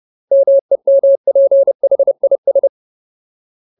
Morse code, heard as a keyed tone, about 30 wpm, 550 Hz. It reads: MEMPHIS
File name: Ultimate Word List / 30WPM Ultimate